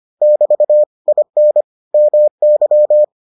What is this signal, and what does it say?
Morse code, heard as a keyed tone, about 25 wpm, 600 Hz. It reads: <BT> IN MY